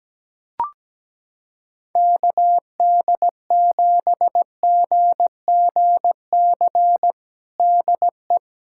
Morse code, heard as a keyed tone, about 17 wpm, 700 Hz. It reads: KD7GGC DE
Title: USA Callsign DE Callsign - 6 Characters - 17wpm